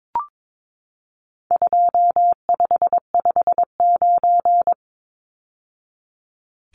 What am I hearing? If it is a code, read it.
Morse code, heard as a keyed tone, about 22 wpm, 700 Hz. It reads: 2559
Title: Numbers - 4-Digit 22wpm